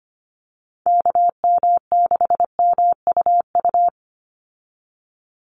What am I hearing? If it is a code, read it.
Morse code, heard as a keyed tone, about 25 wpm, 700 Hz. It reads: KM6MUU
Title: USA Call Signs 25wpm